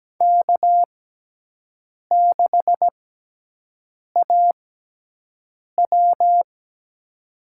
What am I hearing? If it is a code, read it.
Morse code, heard as a keyed tone, about 17 wpm, spaced as for 6 wpm, 700 Hz. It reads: K6AW
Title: USA Call Signs - 6x Character Spacing - 17wpm